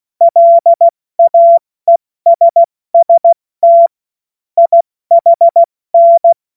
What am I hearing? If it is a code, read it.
Morse code, heard as a keyed tone, about 16 wpm, 680 Hz. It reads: LAESST IHN
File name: Morsekurs / Witze / Witz 20. September 2022 70 F80